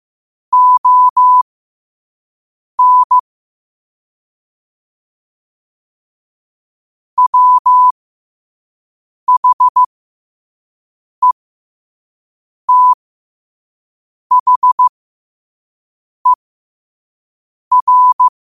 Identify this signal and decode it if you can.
Morse code, heard as a keyed tone, about 15 wpm, spaced as for 5 wpm, 1000 Hz. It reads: ON WHETHER
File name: qotdfiles / 5wpm